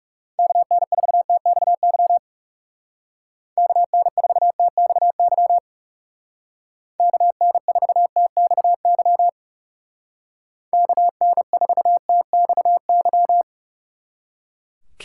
Morse code, heard as a keyed tone, about 36 wpm, 700 Hz. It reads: KN4TXY KN4TXY KN4TXY KN4TXY
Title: Speed Racing - USA 2x3 Call Signs 30wpm